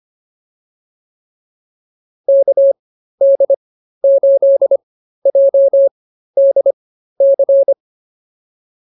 Morse code, heard as a keyed tone, about 25 wpm, spaced as for 13 wpm, 590 Hz. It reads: KD8JDC
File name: POTA110 13